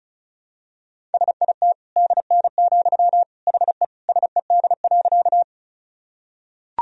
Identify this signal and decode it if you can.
Morse code, heard as a keyed tone, about 34 wpm, 700 Hz. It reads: SIT DN, HE SED.
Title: Sentences from Top 500 Words and Abbreviations 35wpm